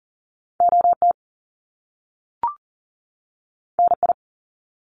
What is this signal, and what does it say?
Morse code, heard as a keyed tone, about 40 wpm, 700 Hz. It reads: OT NI
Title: Top Two Letter Combinations in Words 40wpm